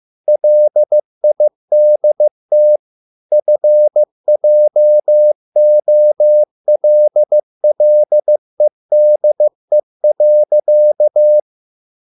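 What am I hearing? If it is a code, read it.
Morse code, heard as a keyed tone, about 15 wpm, 600 Hz. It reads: LIDT FJOLLEDE.
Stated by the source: Morse code at a slow 15 wpm from CW med Gnister.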